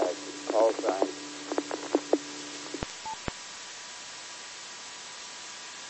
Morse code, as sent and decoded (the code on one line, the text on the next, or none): none